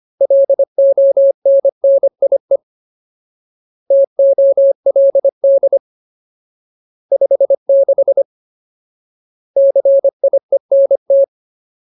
Morse code, heard as a keyed tone, about 25 wpm, 550 Hz. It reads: LONNIE TOLD 56 CIENT
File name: Ultimate Word List / 25WPM Ultimate